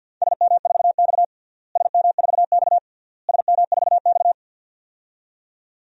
Morse code, heard as a keyed tone, about 47 wpm, 700 Hz. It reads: SM4X SM4X SM4X
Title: DXCC-100 Call Signs - 4 Characters - Repeated 3x - 50wpm